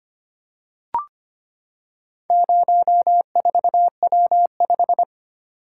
Morse code, heard as a keyed tone, about 25 wpm, 700 Hz. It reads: 04W5
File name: practice / 25wpm / Four Letter-Number Combinations - 25wpm